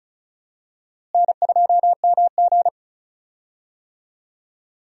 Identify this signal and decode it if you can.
Morse code, heard as a keyed tone, about 36 wpm, 700 Hz. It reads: N2MG
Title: USA Call Signs - 4 Characters - Repeated 3x - 35wpm